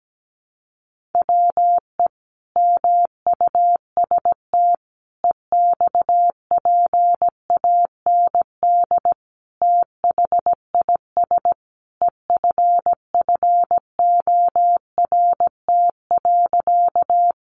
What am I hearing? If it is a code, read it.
Morse code, heard as a keyed tone, about 17 wpm, 700 Hz. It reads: WE MUST EXPAND THIS EFFORT.
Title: Sentences from Top 1400 Words 17wpm